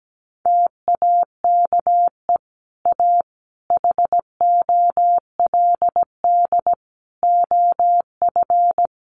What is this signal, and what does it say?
Morse code, heard as a keyed tone, about 17 wpm, 700 Hz. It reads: TAKE A HOLD OF
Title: Sentences from Top 400 Words 17wpm